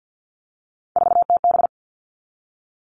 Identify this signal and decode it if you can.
Morse code, heard as a keyed tone, about 50 wpm, 700 Hz. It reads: 4TB
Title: All Cut Numbers - 3-Digit 50wpm